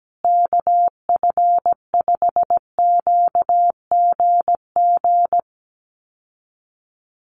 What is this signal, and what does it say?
Morse code, heard as a keyed tone, about 17 wpm, 700 Hz. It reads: KF5QGG
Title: USA Call Signs Repeated 2x 17wpm